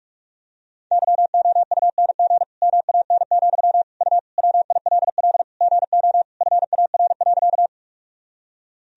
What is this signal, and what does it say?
Morse code, heard as a keyed tone, about 41 wpm, 700 Hz. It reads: YOUNG MAN, U WILL GO FAR.
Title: Sentences from Top 400 Words and Abbreviations 45wpm